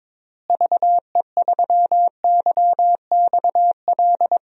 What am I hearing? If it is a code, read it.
Morse code, heard as a keyed tone, about 23 wpm, 700 Hz. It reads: VE3YXL
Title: DXCC-100 Call Signs - 6 Characters - Repeated 2x - 22wpm